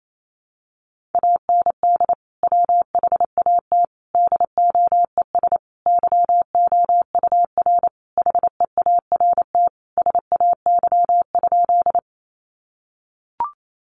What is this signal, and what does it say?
Morse code, heard as a keyed tone, about 28 wpm, 700 Hz. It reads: AND WHAT DOES YOUR HEART SAY?
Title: Sentences from Top 1200 Words 28wpm